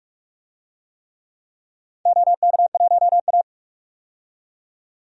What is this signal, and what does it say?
Morse code, heard as a keyed tone, about 46 wpm, 700 Hz. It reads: OK1A
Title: DXCC-100 Call Signs - 4 Characters - 45wpm